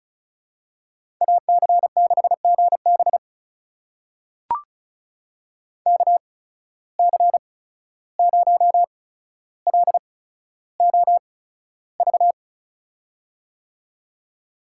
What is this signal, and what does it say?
Morse code, heard as a keyed tone, about 36 wpm, spaced as for 14 wpm, 700 Hz. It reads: AC6GB KC0LOV
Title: USA Call Signs - 8x Character Spacing - 35wpm